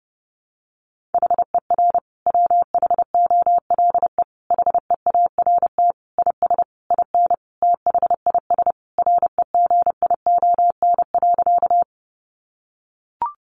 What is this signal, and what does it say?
Morse code, heard as a keyed tone, about 30 wpm, 700 Hz. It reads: HER WHOLE HEART IS IN THIS REGION.
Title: Sentences from Top 1500 Words 30wpm